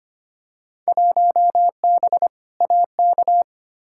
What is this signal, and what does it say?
Morse code, heard as a keyed tone, about 25 wpm, 700 Hz. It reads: 1B AK
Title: Field Day Exchanges Encore - 25wpm